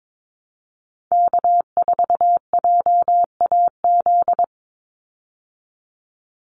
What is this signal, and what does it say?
Morse code, heard as a keyed tone, about 22 wpm, 700 Hz. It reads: K4JAZ
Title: Call Signs Repeated 3x - 22wpm